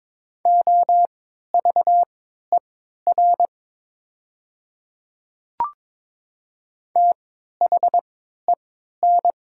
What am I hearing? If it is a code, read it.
Morse code, heard as a keyed tone, about 22 wpm, spaced as for 13 wpm, 700 Hz. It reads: OVER THEN
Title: 3x Character Spacing - Top 200 Words - 22wpm